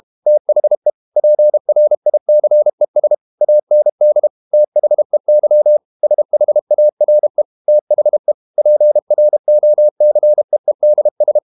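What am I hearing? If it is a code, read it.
Morse code, heard as a keyed tone, about 32 wpm, 600 Hz. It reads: THE PRICES AND THEY SHARE THE PROCEEDS
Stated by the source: Morse code practice at 32 wpm from CW med Gnister, reading Ken Follett's novel Never in English.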